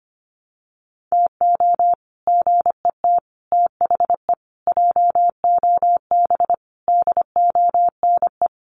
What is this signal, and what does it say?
Morse code, heard as a keyed tone, about 25 wpm, 700 Hz. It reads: TO GET THE JOB DONE
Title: Sets of 5 Words Encore2 25wpm